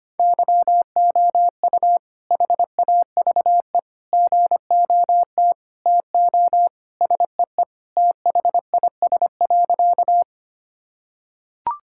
Morse code, heard as a keyed tone, about 25 wpm, 700 Hz. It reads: YOU HAVE GOT TO SEE THIS.
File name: practice / 25wpm / Sentences with 2-Word Phrases - 25wpm